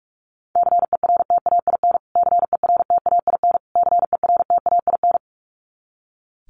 Morse code, heard as a keyed tone, about 44 wpm, 700 Hz. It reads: CERTAIN CERTAIN CERTAIN
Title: Top 500 Words Repeated 45wpm